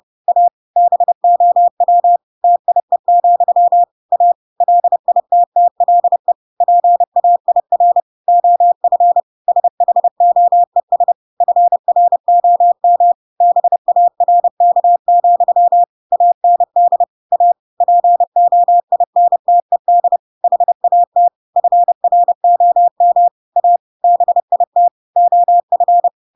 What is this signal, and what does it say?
Morse code, heard as a keyed tone, about 30 wpm, 700 Hz. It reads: A BOW TIE, A LITTLE PAIR OF SHOES FROM BARK, AND A POINTED HAT FROM A BIT OF
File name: Pinocchio / Pinocchio ch8 30 30 ik1hge